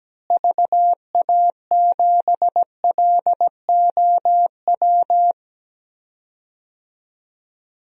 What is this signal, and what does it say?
Morse code, heard as a keyed tone, about 17 wpm, 700 Hz. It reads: VA7LOW
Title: DXCC-100 Call Signs - 6 Characters - Repeated 3x - 17wpm